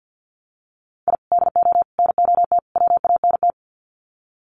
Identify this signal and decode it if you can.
Morse code, heard as a keyed tone, about 50 wpm, 700 Hz. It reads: I DO NOT WANT
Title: Sets of 4 Words 50wpm